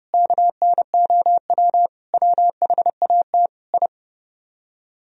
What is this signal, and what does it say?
Morse code, heard as a keyed tone, about 28 wpm, 700 Hz. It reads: KNOW WHAT I